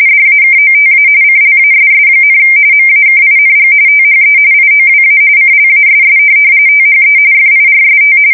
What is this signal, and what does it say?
RTTY, about 45.45 baud. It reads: DAKOTA CLASS, MAKING THE IOWA CLASS THE FIRST
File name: fsk170 / USS-New-Jersey